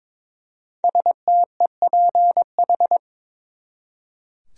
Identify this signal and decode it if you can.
Morse code, heard as a keyed tone, about 22 wpm, 700 Hz. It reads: STEPH
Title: USA Common Names - 22wpm